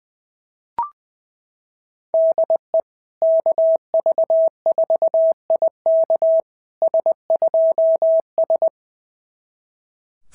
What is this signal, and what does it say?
Morse code, heard as a keyed tone, about 20 wpm, 650 Hz. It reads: DE KV4IK S2S
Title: SOTA QSO Practice 20wpm